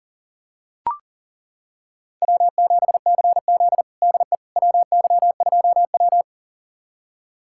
Morse code, heard as a keyed tone, about 40 wpm, 700 Hz. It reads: W7CZ DE WY2W
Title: USA Callsign DE Callsign - 4 Characters - 40wpm